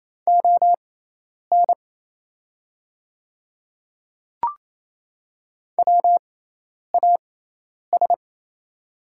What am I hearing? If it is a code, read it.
Morse code, heard as a keyed tone, about 28 wpm, spaced as for 10 wpm, 690 Hz. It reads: ON WAS
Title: 6x Character Spacing - Top 100 Words - 28wpm